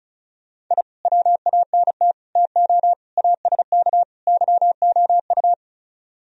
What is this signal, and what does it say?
Morse code, heard as a keyed tone, about 38 wpm, 700 Hz. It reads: I WANT TO ASK YOU